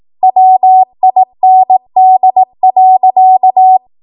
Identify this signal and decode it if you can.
Morse code, heard as a keyed tone, about 18 wpm, 750 Hz. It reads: WIND.
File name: Morse / 18WPM